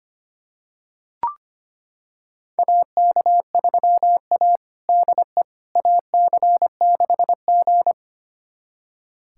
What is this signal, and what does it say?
Morse code, heard as a keyed tone, about 25 wpm, 700 Hz. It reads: AK3A DE AC6G
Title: USA Callsign DE Callsign - 4 Characters - 25wpm